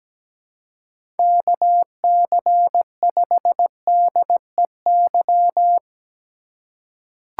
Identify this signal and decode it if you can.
Morse code, heard as a keyed tone, about 17 wpm, 700 Hz. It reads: KC5DEY
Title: USA Call Signs - 6 Characters - 4x Character Spacing - 17wpm